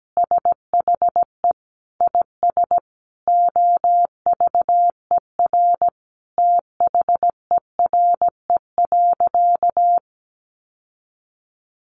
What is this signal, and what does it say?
Morse code, heard as a keyed tone, about 17 wpm, 700 Hz. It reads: SHE IS OVER THERE.
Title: Sentences from Top 600 Words 17wpm